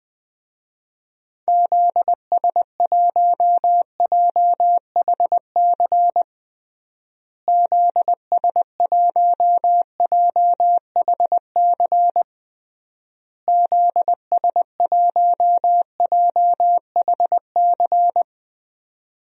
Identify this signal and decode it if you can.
Morse code, heard as a keyed tone, about 20 wpm, 700 Hz. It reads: ZS1JHC ZS1JHC ZS1JHC
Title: DXCC-100 Call Signs - 6 Characters - Repeated 3x - 20wpm